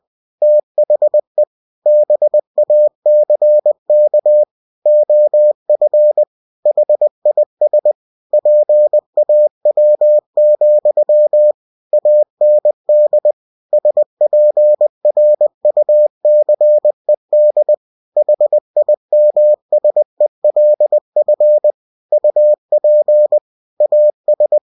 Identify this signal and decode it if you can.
Morse code, heard as a keyed tone, about 20 wpm, 600 Hz. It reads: THE BACK OF HIS PAW, AND SPRUCED HIMSELF UP AS
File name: cw-books / winnie-the-pooh / winnie-20wpm-page-0076